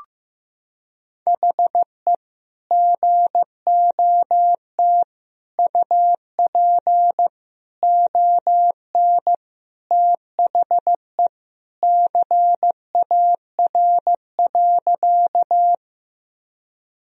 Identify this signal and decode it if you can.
Morse code, heard as a keyed tone, about 15 wpm, 700 Hz. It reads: HE GOT UP ON THE CAR.
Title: Sentences from Top 1000 Words - Review Part 1 - 15wpm